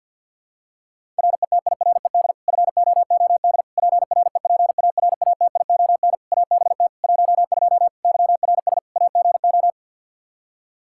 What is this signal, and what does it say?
Morse code, heard as a keyed tone, about 52 wpm, 700 Hz. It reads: RETIRED FOOD PREPARATION ABT 12 YRS AGO
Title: K1USN SST QSO Practice 50wpm